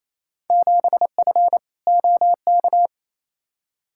Morse code, heard as a keyed tone, about 28 wpm, 700 Hz. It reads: 7F OK
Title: Field Day Exchanges Encore - 28wpm